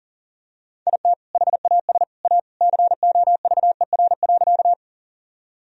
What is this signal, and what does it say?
Morse code, heard as a keyed tone, about 40 wpm, 700 Hz. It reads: IT HAS A COVER.
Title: Sentences from Top 1000 Words - Review Part 1 - 40wpm